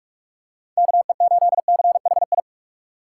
Morse code, heard as a keyed tone, about 44 wpm, 700 Hz. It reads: KE8CHI
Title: USA Call Signs - 6 Characters - 8x Character Spacing - 45wpm